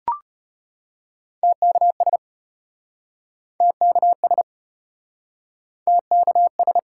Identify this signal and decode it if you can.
Morse code, heard as a keyed tone, about 33 wpm, 700 Hz. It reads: TKS TKS TKS